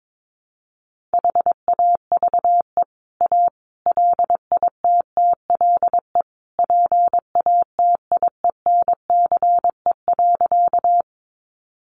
Morse code, heard as a keyed tone, about 22 wpm, 700 Hz. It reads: HAVE A LITTLE PATIENCE.
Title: Sentences from Top 1300 Words 22wpm